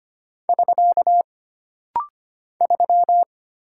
Morse code, heard as a keyed tone, about 25 wpm, 700 Hz. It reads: <SK> 3
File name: Single Character (Letter:Number:Punctuation:Prosign) - Rapid-Fire - 25wpm